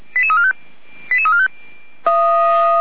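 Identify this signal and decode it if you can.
Five-tone selective call (ZVEI), heard as quick runs of five tones, tones of about 70 ms. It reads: 88345
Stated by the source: ZVEI five-tone call 88263 received on SDR.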